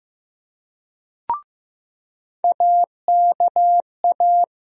Morse code, heard as a keyed tone, about 15 wpm, 700 Hz. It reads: AKA
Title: USA Call Sign Postfixes - 3 Letters 15wpm